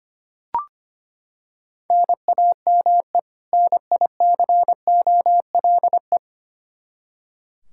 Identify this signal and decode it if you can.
Morse code, heard as a keyed tone, about 25 wpm, 700 Hz. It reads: NAME NICOLE
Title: Practice QSO - Exchange 1 - Beginning - 25wpm